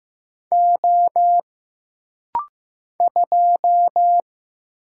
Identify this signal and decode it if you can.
Morse code, heard as a keyed tone, about 15 wpm, 700 Hz. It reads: O 2
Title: Single Letter-Number Alt - Rapid-Fire 15wpm